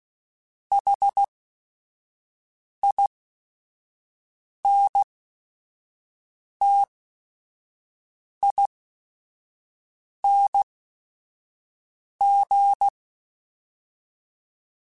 Morse code, heard as a keyed tone, about 16 wpm, spaced as for 5 wpm, 800 Hz. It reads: HINTING
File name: Special INGs / Special INGs-mp3s / Sngl Ing @5